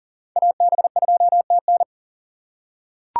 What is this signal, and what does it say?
Morse code, heard as a keyed tone, about 40 wpm, 690 Hz. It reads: AB2TN
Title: USA Call Signs - 5 Characters - 3x Character Spacing - 40wpm